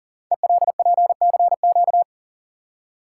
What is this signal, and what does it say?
Morse code, heard as a keyed tone, about 39 wpm, 690 Hz. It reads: ELPCQ
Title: Code Groups - 5 Letters - 40wpm